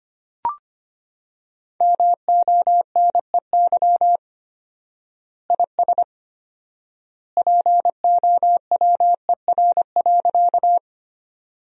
Morse code, heard as a keyed tone, about 25 wpm, 700 Hz. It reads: MONEY IS POWER.
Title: 4x Word-Spacing - Sentences from Top 500 Words - 25wpm